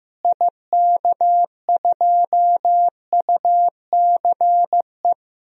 Morse code, heard as a keyed tone, about 15 wpm, 700 Hz. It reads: IK2UCE
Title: DXCC-100 Call Signs - 6 Characters - Repeated 3x - 15wpm